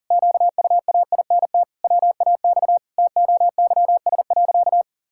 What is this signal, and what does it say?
Morse code, heard as a keyed tone, about 40 wpm, 700 Hz. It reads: QUAINT WAX TOYS.
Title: Pangram Challenge v2 - 40wpm